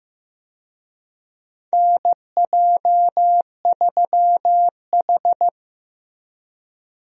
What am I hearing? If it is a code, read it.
Morse code, heard as a keyed tone, about 15 wpm, 700 Hz. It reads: NJ3H
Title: USA Call Signs - 4 Characters - Repeated 3x - 15wpm